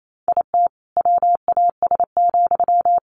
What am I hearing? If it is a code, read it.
Morse code, heard as a keyed tone, about 28 wpm, 700 Hz. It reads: IT WAS,
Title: Sentences from Top 1400 Words 28wpm